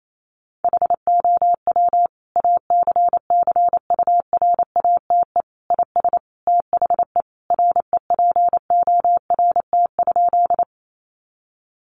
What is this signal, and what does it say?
Morse code, heard as a keyed tone, about 28 wpm, 700 Hz. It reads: HOW ACCURATE IS THE REPORT?
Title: Sentences from Top 1200 Words 28wpm